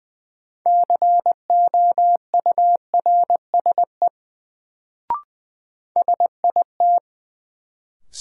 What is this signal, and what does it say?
Morse code, heard as a keyed tone, about 20 wpm, 700 Hz. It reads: COURSE SIT
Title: Sentences from Top 500 Words - New Word Drill - 20wpm